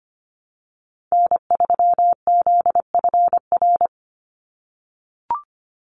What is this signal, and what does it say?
Morse code, heard as a keyed tone, about 25 wpm, 700 Hz. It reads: N3ZFR
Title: USA Call Signs 25wpm